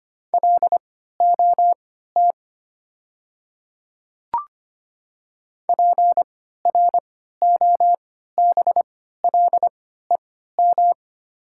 Morse code, heard as a keyed tone, about 25 wpm, spaced as for 14 wpm, 700 Hz. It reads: LOT PROBLEM
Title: 3x Character Spacing - Top 500 Words - 25wpm